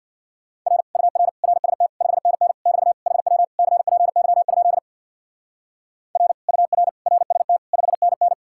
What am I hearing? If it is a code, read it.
Morse code, heard as a keyed tone, about 58 wpm, 700 Hz. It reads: R UR RST 5NN <BT> HW CPY? R UR RST 5NN
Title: Speed-Racing Common QSO Phrases Encore - 50wpm